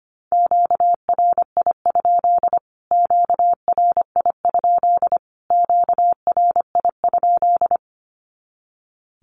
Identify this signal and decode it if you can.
Morse code, heard as a keyed tone, about 25 wpm, 700 Hz. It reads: QRI? QRI? QRI?
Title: Q-Codes Repeated 3x 25wpm